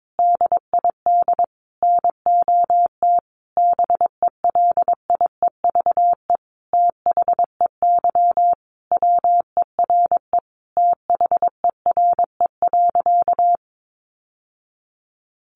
Morse code, heard as a keyed tone, about 22 wpm, 700 Hz. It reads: DID NOT BELIEVE THEY WERE THERE.
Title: Sentences from Top 1000 Words 22wpm